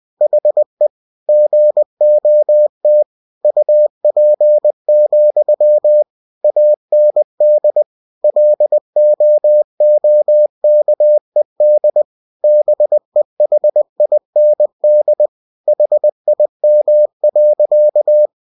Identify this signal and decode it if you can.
Morse code, heard as a keyed tone, about 20 wpm, 600 Hz. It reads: HE GOT UP, AND LOOKED BEHIND HIM.
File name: cw-books / winnie-the-pooh / winnie-20wpm-page-0084